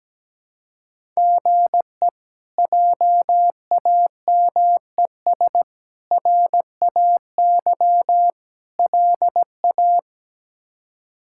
Morse code, heard as a keyed tone, about 17 wpm, 700 Hz. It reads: GE JAMES RAY LA
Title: K1USN SST QSO Practice - 17wpm